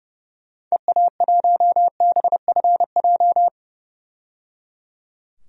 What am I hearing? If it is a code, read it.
Morse code, heard as a keyed tone, about 30 wpm, 700 Hz. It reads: EA1BFJ
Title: DXCC-100 Call Signs - 6 Characters - 30wpm